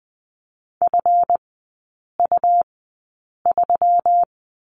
Morse code, heard as a keyed tone, about 20 wpm, spaced as for 8 wpm, 700 Hz. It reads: FU3
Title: 138 Lesson 23e - Letter f - Three Characters 20wpm